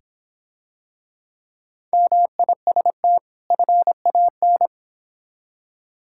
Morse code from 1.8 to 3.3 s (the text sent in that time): MIST